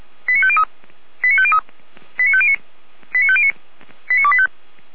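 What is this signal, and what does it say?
Five-tone selective call (ZVEI), heard as quick runs of five tones, tones of about 70 ms. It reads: 89503 89503 89509 89509 89285
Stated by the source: ZVEI five-tone selective call received on SDR, Feuerwehr Orlamuende.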